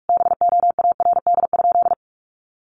Morse code, heard as a keyed tone, about 44 wpm, 700 Hz. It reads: BOARD?
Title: Sentences from Top 1300 Words 45wpm